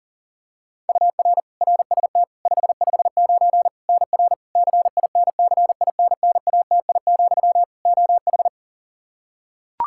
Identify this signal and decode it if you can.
Morse code, heard as a keyed tone, about 40 wpm, 700 Hz. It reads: UR RST 559 NR CINCINNATI, OH
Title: K1USN SST QSO Practice 40wpm